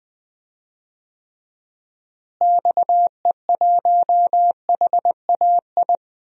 Morse code, heard as a keyed tone, about 20 wpm, 700 Hz. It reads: XE1HAI
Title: DXCC-100 Call Signs - 6 Characters - Repeated 3x - 20wpm